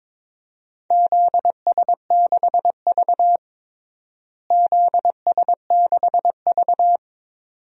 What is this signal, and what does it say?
Morse code, heard as a keyed tone, about 22 wpm, 700 Hz. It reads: ZS6V ZS6V
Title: DXCC-100 Call Signs - 4 Characters - Repeated 2x - 22wpm